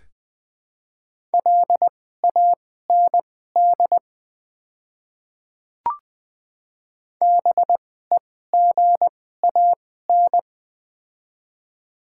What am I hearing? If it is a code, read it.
Morse code, heard as a keyed tone, about 21 wpm, spaced as for 15 wpm, 700 Hz. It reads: LAND BEGAN
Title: 2x Character Spacing - Top 300 Words - 20wpm